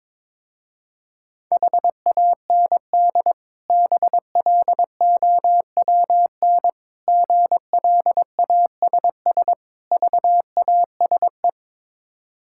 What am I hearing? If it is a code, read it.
Morse code, heard as a keyed tone, about 22 wpm, 690 Hz. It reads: HAND BLOWN GLASS VASE
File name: Difficult Words with Hints - 22wpm